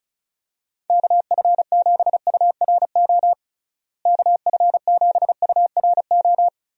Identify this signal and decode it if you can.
Morse code, heard as a keyed tone, about 36 wpm, 700 Hz. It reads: KF7URO KF7URO
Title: USA Call Signs - 6 Characters - Repeated 2x - 35wpm